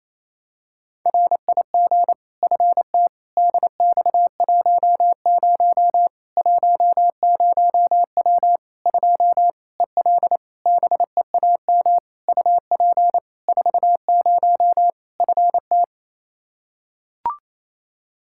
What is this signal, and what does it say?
Morse code, heard as a keyed tone, about 28 wpm, 700 Hz. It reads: RIG FT DX10 10W 2 EL BEAM UP 40 FT
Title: Practice QSO - Exchange 2 - Info - 28wpm